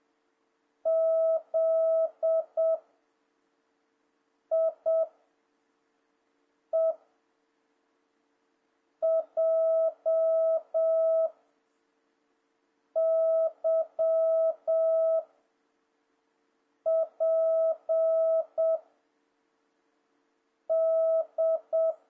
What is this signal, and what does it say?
Morse code, heard as a keyed tone, about 7 wpm, 650 Hz. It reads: Z I E J Y P D
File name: jh8cbh / 2026cw